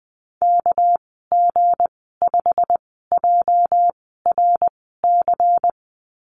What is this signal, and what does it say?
Morse code, heard as a keyed tone, about 20 wpm, spaced as for 15 wpm, 700 Hz. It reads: KG5JRC
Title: 231 Lesson 37g - Letter x - Call Signs 20wpm